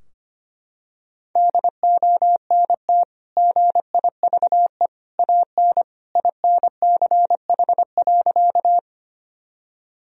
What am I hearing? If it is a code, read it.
Morse code, heard as a keyed tone, about 25 wpm, 700 Hz. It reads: DONT GIVE AN INCH.
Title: Sentences from Top 1000 Words - Review Part 3 - 25wpm